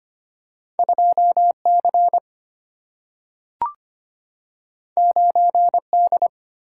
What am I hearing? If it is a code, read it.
Morse code, heard as a keyed tone, about 25 wpm, 700 Hz. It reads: 2C 9D
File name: Two Letter-Number Combinations - 25wpm